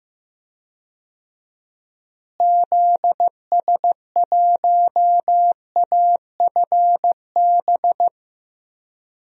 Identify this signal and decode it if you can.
Morse code, heard as a keyed tone, about 15 wpm, 700 Hz. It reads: ZS1AFB
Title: DXCC-100 Call Signs - 6 Characters - Repeated 3x - 15wpm